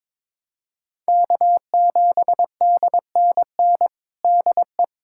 Morse code, heard as a keyed tone, about 22 wpm, 700 Hz. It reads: K7DNN DE
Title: USA Callsign DE Callsign - 5 Characters - 22wpm